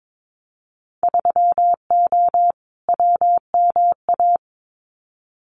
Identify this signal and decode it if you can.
Morse code, heard as a keyed tone, about 22 wpm, 700 Hz. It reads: 3O WMA
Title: Winter Field Day Exchanges Encore 22wpm